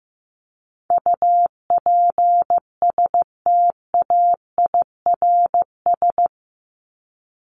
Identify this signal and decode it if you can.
Morse code, heard as a keyed tone, about 15 wpm, 700 Hz. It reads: UPSTAIRS
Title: Top 1751-2000 Words 15wpm